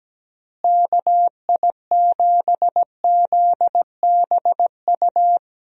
Morse code, heard as a keyed tone, about 17 wpm, 700 Hz. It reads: KI7ZBU
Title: USA Call Signs - 6 Characters - Repeated 3x - 17wpm